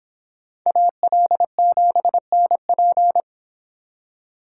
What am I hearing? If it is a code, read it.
Morse code, heard as a keyed tone, about 26 wpm, 700 Hz. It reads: AL7NP